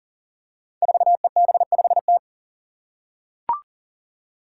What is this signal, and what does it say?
Morse code, heard as a keyed tone, about 40 wpm, 700 Hz. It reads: 4EB5T
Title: Five Challenging Letter-Number Combinations 40wpm